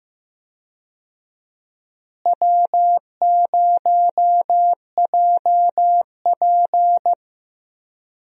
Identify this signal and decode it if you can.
Morse code, heard as a keyed tone, about 15 wpm, 700 Hz. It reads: W0JP
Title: USA Call Signs - 4 Characters - Repeated 2x - 15wpm